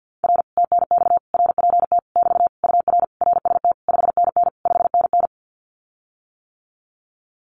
Morse code, heard as a keyed tone, about 50 wpm, 700 Hz. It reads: R TNX RPT <BT> UR RST 5NN 5NN
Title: Common QSO Phrases Encore 50wpm